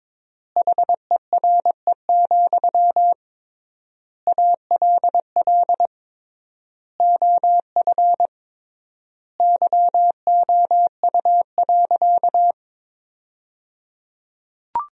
Morse code, heard as a keyed tone, about 21 wpm, 700 Hz. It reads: HERE, ALL OF YOU.
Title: 3x Word-Spacing - Sentences from Top 500 Words - 22wpm